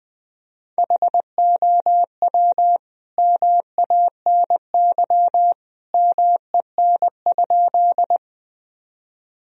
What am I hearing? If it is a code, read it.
Morse code, heard as a keyed tone, about 21 wpm, 700 Hz. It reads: HOW MANY MEN?
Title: Sentences from Top 1000 Words - Review Part 1 - 20wpm